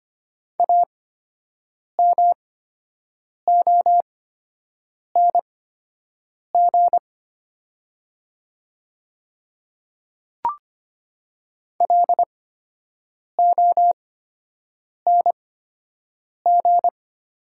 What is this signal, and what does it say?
Morse code, heard as a keyed tone, about 25 wpm, spaced as for 7 wpm, 700 Hz. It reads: AMONG LONG
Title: 8x Character Spacing - Top 500 Words - 25wpm